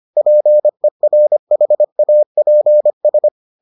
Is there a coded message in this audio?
Yes: Morse code from 0.2 to 3.3 s, heard as a keyed tone, about 25 wpm, 600 Hz. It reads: PERHAPS